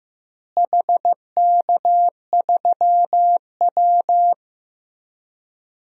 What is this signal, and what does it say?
Morse code, heard as a keyed tone, about 15 wpm, 700 Hz. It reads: HK3W
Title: DXCC-100 Call Signs - 4 Characters - Repeated 3x - 15wpm